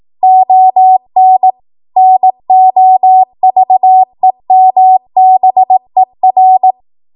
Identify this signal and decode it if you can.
Morse code, heard as a keyed tone, about 18 wpm, 750 Hz. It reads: ON NOVEMBER
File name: Archive / 18 WPM / 18WPM